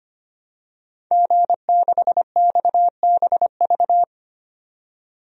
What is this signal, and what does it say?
Morse code, heard as a keyed tone, about 25 wpm, 700 Hz. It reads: G6XBV
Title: DXCC-100 Call Signs - 5 Characters - 25wpm